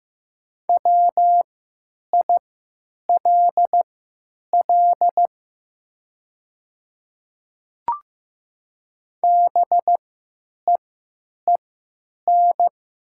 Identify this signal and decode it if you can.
Morse code, heard as a keyed tone, about 15 wpm, spaced as for 9 wpm, 700 Hz. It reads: WILL BEEN
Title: 3x Character Spacing - Top 100 Words - 15wpm